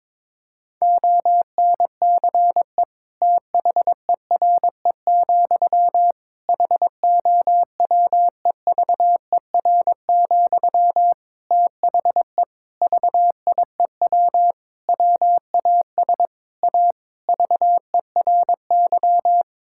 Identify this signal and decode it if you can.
Morse code, heard as a keyed tone, about 22 wpm, 700 Hz. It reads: ONCE THERE, HOWEVER, THE VIEW WAS A VERY
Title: Sentences from Top 1000 Words - Review Part 6 - 22wpm